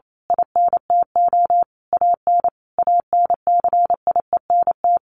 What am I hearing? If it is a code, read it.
Morse code, heard as a keyed tone, about 28 wpm, 700 Hz. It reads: INTO AN ANCIENT
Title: A Sound Of Thunder 28wpm